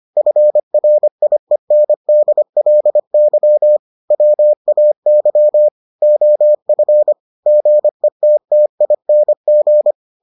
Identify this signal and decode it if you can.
Morse code, heard as a keyed tone, about 25 wpm, 600 Hz. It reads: FRIENDLY WAY OF GETTING